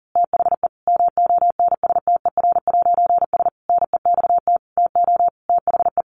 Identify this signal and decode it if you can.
Morse code, heard as a keyed tone, about 40 wpm, 700 Hz. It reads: THE MONSTER'S NEXT TO THE
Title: A Sound Of Thunder 40wpm